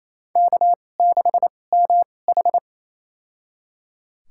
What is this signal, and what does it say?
Morse code, heard as a keyed tone, about 27 wpm, spaced as for 20 wpm, 690 Hz. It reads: K6MH
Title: USA Call Signs - 4 Characters - 2x Character Spacing - 28wpm